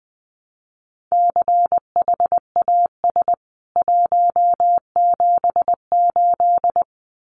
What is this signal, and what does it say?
Morse code, heard as a keyed tone, about 20 wpm, 700 Hz. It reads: CHAS 178
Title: CWT Exchange 20wpm